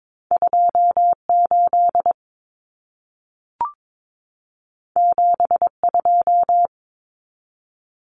Morse code, heard as a keyed tone, about 22 wpm, 700 Hz. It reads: 28 72
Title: Numbers - 2-Digit 22wpm